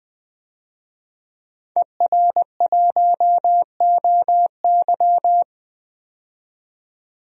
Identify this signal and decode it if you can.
Morse code, heard as a keyed tone, about 20 wpm, 700 Hz. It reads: ER1OY
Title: DXCC-100 Call Signs - 5 Characters - 20wpm